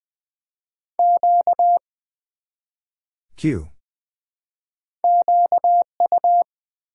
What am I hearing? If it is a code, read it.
Morse code, heard as a keyed tone, about 20 wpm, 700 Hz. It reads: Q QU
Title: Build A Word - 1 Letter At A Time - 5-Letter Words - 20wpm